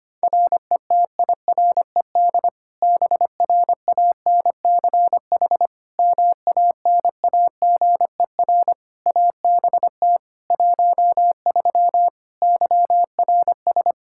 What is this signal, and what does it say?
Morse code, heard as a keyed tone, about 25 wpm, 700 Hz. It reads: RETIRED BRANCH MANAGER ABT 13 YRS
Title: K1USN SST QSO Practice 25wpm